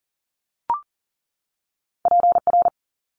Morse code, heard as a keyed tone, about 42 wpm, 700 Hz. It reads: PR
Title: ARRL Sections 40wpm